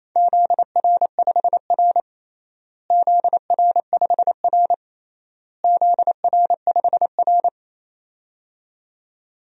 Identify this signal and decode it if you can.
Morse code, heard as a keyed tone, about 28 wpm, 700 Hz. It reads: ZR5R ZR5R ZR5R
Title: DXCC-100 Call Signs - 4 Characters - Repeated 3x - 28wpm